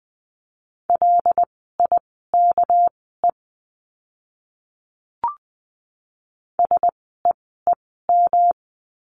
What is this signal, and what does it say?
Morse code, heard as a keyed tone, about 20 wpm, spaced as for 14 wpm, 700 Hz. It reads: LIKE SEEM
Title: 244 Lesson 39f - Letter z - Words 20wpm